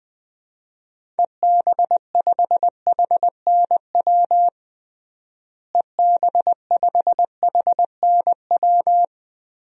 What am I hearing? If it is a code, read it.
Morse code, heard as a keyed tone, about 20 wpm, 700 Hz. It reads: EB5HNW EB5HNW
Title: DXCC-100 Call Signs - 6 Characters - Repeated 2x - 20wpm